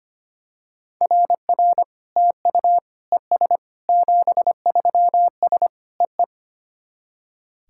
Morse code, heard as a keyed tone, about 25 wpm, 700 Hz. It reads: RR TU ES 73S EE
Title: SOTA QSO Practice 25wpm